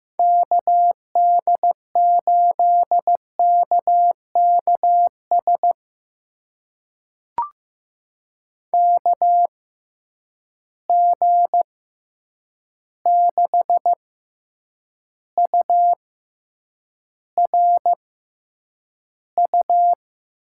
Morse code, heard as a keyed tone, about 15 wpm, spaced as for 8 wpm, 700 Hz. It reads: KD8KKS KG6URU
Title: USA Call Signs - 6 Characters - 6x Character Spacing - 15wpm